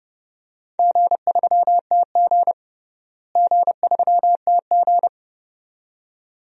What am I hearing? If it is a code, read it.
Morse code, heard as a keyed tone, about 30 wpm, 700 Hz. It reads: G3TG G3TG
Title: DXCC-100 Call Signs - 4 Characters - Repeated 2x - 30wpm